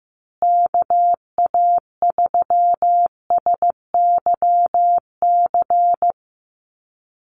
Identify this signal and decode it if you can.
Morse code, heard as a keyed tone, about 15 wpm, 700 Hz. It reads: KA3SYC
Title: USA Call Signs Repeated 2x 15wpm